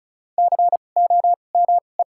Morse code, heard as a keyed tone, about 36 wpm, spaced as for 25 wpm, 700 Hz. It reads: COME
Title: 2x Character Spacing - Top 100 Words - 35wpm